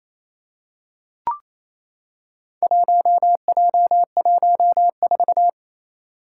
Morse code, heard as a keyed tone, about 27 wpm, 700 Hz. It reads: 1J14
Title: Four Challenging Letter-Number Combinations 28wpm